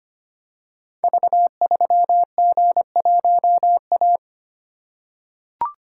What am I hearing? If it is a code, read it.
Morse code, heard as a keyed tone, about 25 wpm, 700 Hz. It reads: V3G1A
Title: Code Groups - 5 Letter:Numbers - 25wpm